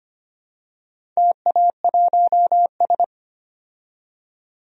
Morse code, heard as a keyed tone, about 25 wpm, 700 Hz. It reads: TA1S
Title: DXCC-100 Call Signs - 4 Characters - 25wpm